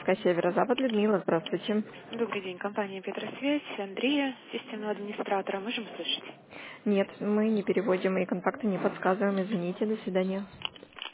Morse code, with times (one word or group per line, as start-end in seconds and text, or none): none